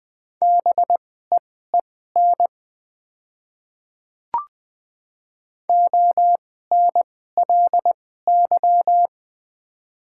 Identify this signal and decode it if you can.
Morse code, heard as a keyed tone, about 20 wpm, spaced as for 14 wpm, 700 Hz. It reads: BEEN ONLY
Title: 2x Character Spacing - Top 200 Words - 20wpm